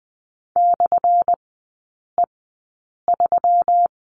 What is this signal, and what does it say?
Morse code, heard as a keyed tone, about 20 wpm, spaced as for 8 wpm, 700 Hz. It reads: /E3
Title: 178 Lesson 29e - Slash - Three Characters 20wpm